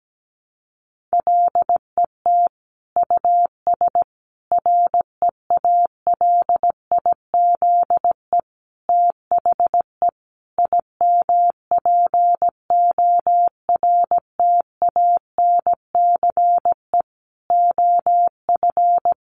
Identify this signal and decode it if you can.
Morse code, heard as a keyed tone, about 17 wpm, 700 Hz. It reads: LET US REALIZE THE IMPORTANCE OF
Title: Sentences from Top 1300 Words 17wpm